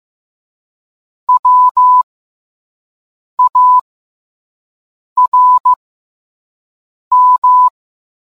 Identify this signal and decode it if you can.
Morse code, heard as a keyed tone, about 15 wpm, spaced as for 5 wpm, 1000 Hz. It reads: WARM